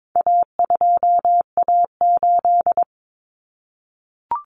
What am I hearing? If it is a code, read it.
Morse code, heard as a keyed tone, about 22 wpm, 700 Hz. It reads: A2A8
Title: Common Cut Numbers - 4-Digit 22wpm